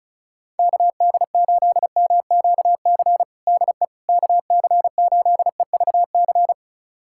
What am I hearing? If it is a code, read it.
Morse code, heard as a keyed tone, about 36 wpm, 700 Hz. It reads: KD8MQC DE KC8EVC
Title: USA Callsign DE Callsign 35wpm